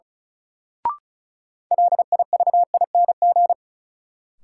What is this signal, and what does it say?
Morse code, heard as a keyed tone, about 36 wpm, 700 Hz. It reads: LIVING
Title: Sentences from Top 1000 Words - Review Part 6 - New Word Drill - 35wpm